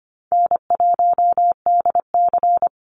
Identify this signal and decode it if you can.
Morse code, heard as a keyed tone, about 25 wpm, 700 Hz. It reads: N1DC
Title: Call Signs - 25wpm